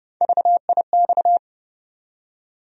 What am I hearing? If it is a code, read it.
Morse code, heard as a keyed tone, about 30 wpm, 700 Hz. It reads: VIX
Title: USA Call Sign Postfixes - 3 Letters 30wpm